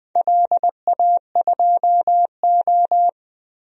Morse code, heard as a keyed tone, about 20 wpm, 700 Hz. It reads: LA2O
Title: DXCC-100 Call Signs - 4 Characters - Repeated 3x - 20wpm